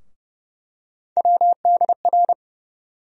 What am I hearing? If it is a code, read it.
Morse code, heard as a keyed tone, about 30 wpm, 700 Hz. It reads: WDR
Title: USA Call Sign Postfixes - 3 Letters 30wpm